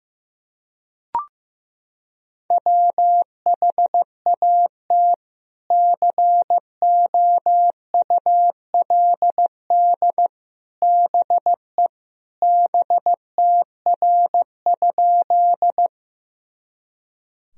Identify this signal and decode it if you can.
Morse code, heard as a keyed tone, about 15 wpm, 700 Hz. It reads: WHAT COULD BE BTR?
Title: Sentences from Top 400 Words and Abbreviations 15wpm